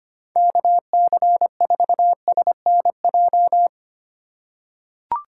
Morse code, heard as a keyed tone, about 25 wpm, 700 Hz. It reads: KC4SNJ
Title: USA Call Signs - 6 Characters - 2x Character Spacing - 25wpm